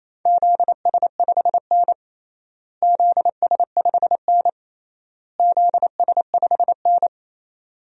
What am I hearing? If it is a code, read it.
Morse code, heard as a keyed tone, about 28 wpm, 700 Hz. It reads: ZS5N ZS5N ZS5N
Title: DXCC-100 Call Signs - 4 Characters - Repeated 3x - 28wpm